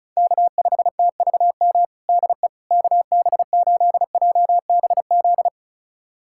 Morse code, heard as a keyed tone, about 34 wpm, 700 Hz. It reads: K5TVM DE KB8JBZ
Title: USA Callsign DE Callsign 35wpm